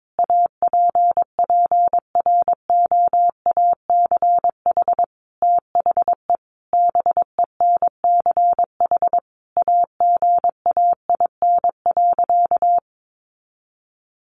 Morse code, heard as a keyed tone, about 22 wpm, 700 Hz. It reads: APPROACH THE BENCH AGAIN.
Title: Sentences from Top 1400 Words 22wpm